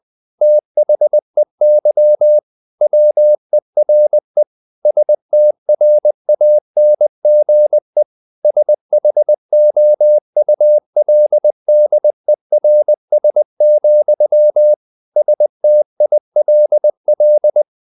Morse code, heard as a keyed tone, about 20 wpm, 600 Hz. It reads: THEY WERE STRANGE SHOULDERS, STILL
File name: the old man and the sea / the old man and the sea part 1 20wpm